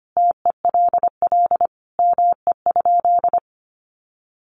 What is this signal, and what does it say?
Morse code, heard as a keyed tone, about 25 wpm, 700 Hz. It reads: TELL ME?
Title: Sentences from Top 1300 Words 25wpm